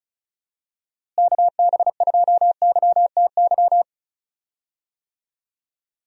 Morse code, heard as a keyed tone, about 34 wpm, 700 Hz. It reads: KB2YTY
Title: USA Call Signs - 6 Characters - Repeated 3x - 35wpm